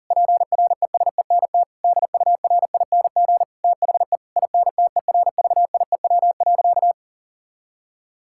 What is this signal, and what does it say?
Morse code, heard as a keyed tone, about 40 wpm, 700 Hz. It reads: PRESENT DURING THE INTERVIEW.
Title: Sentences from Top 1600 Words 40wpm